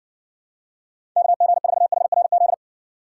Morse code, heard as a keyed tone, about 60 wpm, 700 Hz. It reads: KG4HRZ